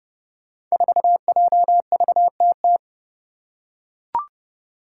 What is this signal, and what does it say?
Morse code, heard as a keyed tone, about 30 wpm, 700 Hz. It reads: 4JVTT
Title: Five Challenging Letter-Number Combinations 30wpm